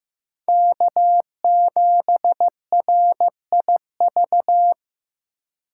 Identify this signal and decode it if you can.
Morse code, heard as a keyed tone, about 15 wpm, 700 Hz. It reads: K7RIV
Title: USA Call Signs - 5 Characters - 4x Character Spacing - 15wpm